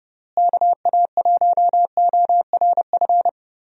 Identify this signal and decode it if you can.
Morse code, heard as a keyed tone, about 30 wpm, 700 Hz. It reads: KA1ORF
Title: USA Call Signs - 6 Characters - Repeated 3x - 30wpm